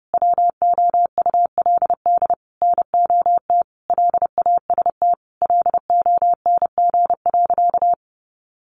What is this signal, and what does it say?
Morse code, heard as a keyed tone, about 30 wpm, 700 Hz. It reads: WOULD NOT LAST LONG.
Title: Sentences from Top 1300 Words 30wpm